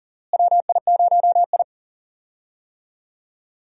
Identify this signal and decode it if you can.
Morse code, heard as a keyed tone, about 40 wpm, 700 Hz. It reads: WI0I
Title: USA Call Signs - 4 Characters - Repeated 2x - 40wpm